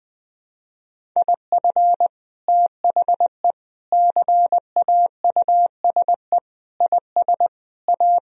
Morse code, heard as a keyed tone, about 20 wpm, 700 Hz. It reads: IF THE CAUSE IS A
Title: Sentences from Top 1000 Words - Review Part 1 - 20wpm